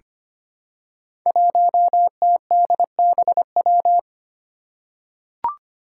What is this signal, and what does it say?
Morse code, heard as a keyed tone, about 25 wpm, 700 Hz. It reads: 1TDBW
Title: Five Challenging Letter-Number Combinations 25wpm